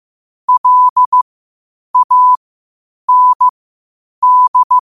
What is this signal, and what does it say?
Morse code, heard as a keyed tone, about 15 wpm, spaced as for 8 wpm, 1000 Hz. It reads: LAND
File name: qotdfiles / 8wpm